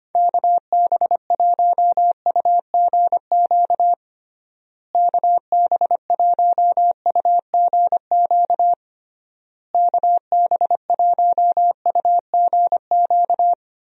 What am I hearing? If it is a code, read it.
Morse code, heard as a keyed tone, about 25 wpm, 700 Hz. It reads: KB1UGQ KB1UGQ KB1UGQ
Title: USA Call Signs - 6 Characters - Repeated 3x - 25wpm